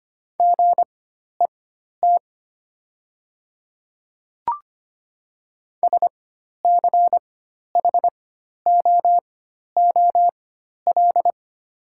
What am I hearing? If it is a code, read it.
Morse code, heard as a keyed tone, about 25 wpm, spaced as for 12 wpm, 700 Hz. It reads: GET SCHOOL
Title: 4x Character Spacing - Top 300 Words - 25wpm